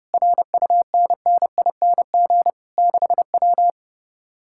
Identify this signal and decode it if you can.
Morse code, heard as a keyed tone, about 30 wpm, 700 Hz. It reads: RUNNING 6W
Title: Speed-Racing Common QSO Phrases - 30wpm